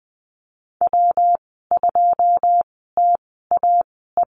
Morse code, heard as a keyed tone, about 20 wpm, spaced as for 15 wpm, 700 Hz. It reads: W2TAE
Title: 161 Lesson 26g - Letter g - Call Signs 20wpm